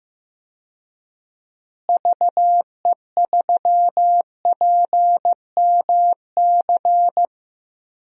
Morse code, heard as a keyed tone, about 15 wpm, 700 Hz. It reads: VE3PMC
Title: DXCC-100 Call Signs - 6 Characters - Repeated 3x - 15wpm